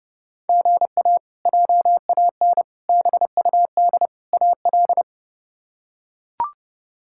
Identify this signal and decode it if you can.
Morse code, heard as a keyed tone, about 29 wpm, 700 Hz. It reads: GA JAN BUD AL
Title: K1USN SST QSO Practice - 30wpm